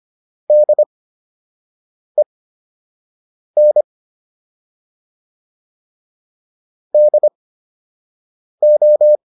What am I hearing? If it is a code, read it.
Morse code, heard as a keyed tone, about 25 wpm, spaced as for 6 wpm, 600 Hz. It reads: DEN DO